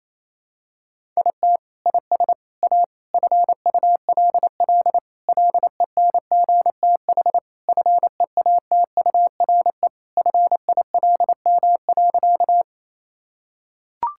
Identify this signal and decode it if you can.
Morse code, heard as a keyed tone, about 28 wpm, 700 Hz. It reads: IT IS A FULL LENGTH FEATURE FILM.
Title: Sentences from Top 1000 Words - Review Part 5 - 28wpm